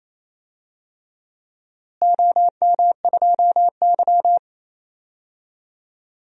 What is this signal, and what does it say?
Morse code, heard as a keyed tone, about 27 wpm, 700 Hz. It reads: OM2Y
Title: DXCC-100 Call Signs - 4 Characters - Repeated 3x - 28wpm